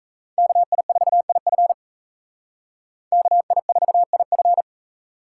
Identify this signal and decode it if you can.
Morse code, heard as a keyed tone, about 35 wpm, 700 Hz. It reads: KI4IF KI4IF